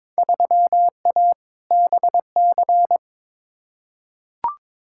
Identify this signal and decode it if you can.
Morse code, heard as a keyed tone, about 22 wpm, 700 Hz. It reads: 3A BC
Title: Field Day Exchanges - 22wpm